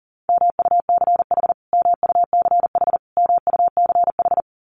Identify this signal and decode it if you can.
Morse code, heard as a keyed tone, about 40 wpm, 700 Hz. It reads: MUCH MUCH MUCH
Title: Top 200 Words Repeated 40wpm